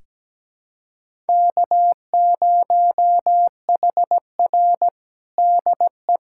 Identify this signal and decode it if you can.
Morse code, heard as a keyed tone, about 17 wpm, 700 Hz. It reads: K0HR DE
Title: USA Callsign DE Callsign - 4 Characters - 17wpm